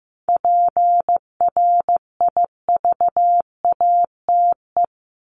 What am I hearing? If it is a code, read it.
Morse code, heard as a keyed tone, about 15 wpm, 700 Hz. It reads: PRIVATE
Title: Top 751-1000 Words 15wpm